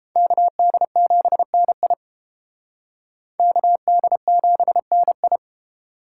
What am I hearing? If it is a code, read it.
Morse code, heard as a keyed tone, about 30 wpm, 700 Hz. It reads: KD7NI KD7NI